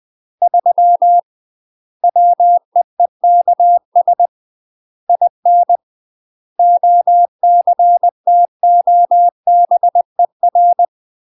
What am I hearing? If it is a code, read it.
Morse code, heard as a keyed tone, about 20 wpm, 700 Hz. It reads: 3 WEEKS IN OCTOBER
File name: UK MASTER QSOs TRAINING / 20WPM / UK MASTER QSOs 0037